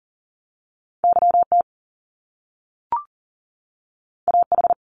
Morse code, heard as a keyed tone, about 40 wpm, 700 Hz. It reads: YT AH
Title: Top Two Letter Combinations in Words 40wpm